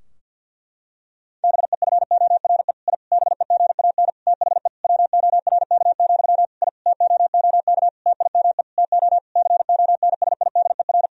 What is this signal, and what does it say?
Morse code, heard as a keyed tone, about 50 wpm, 700 Hz. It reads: BEFORE I BEGAN THE WORK, I TOOK TIME TO CONSIDER
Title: Sentences from Top 1000 Words - Review Part 4 - 50wpm